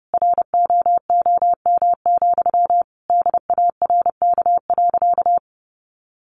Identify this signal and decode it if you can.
Morse code, heard as a keyed tone, about 30 wpm, 700 Hz. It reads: ROOM, DARK.
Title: Sentences from Top 1400 Words 30wpm